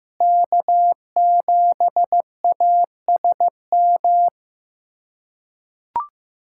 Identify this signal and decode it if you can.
Morse code, heard as a keyed tone, about 15 wpm, 700 Hz. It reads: K7ASM
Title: USA Callsign DE Callsign - 5 Characters - 15wpm